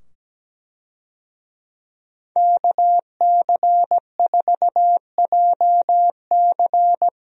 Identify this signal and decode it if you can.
Morse code, heard as a keyed tone, about 17 wpm, 700 Hz. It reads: KC4JC
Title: USA Call Signs - 5 Characters - 17wpm